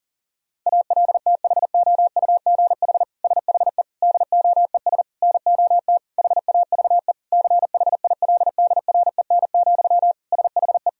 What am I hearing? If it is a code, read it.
Morse code, heard as a keyed tone, about 40 wpm, 700 Hz. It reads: ALTHOUGH SHE DOES NOT HAVE CHILDREN, SHE
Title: Sentences from Top 1000 Words - Review Part 3 - 40wpm